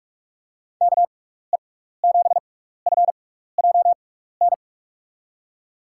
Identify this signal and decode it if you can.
Morse code, heard as a keyed tone, about 45 wpm, spaced as for 15 wpm, 700 Hz. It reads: KE7FJN
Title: USA Call Signs - 6x Character Spacing - 45wpm